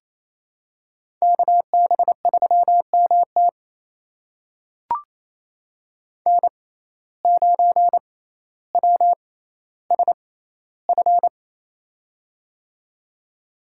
Morse code, heard as a keyed tone, about 28 wpm, spaced as for 14 wpm, 700 Hz. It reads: KB3MT N9WSF
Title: USA Call Signs - 5 Characters - 6x Character Spacing - 28wpm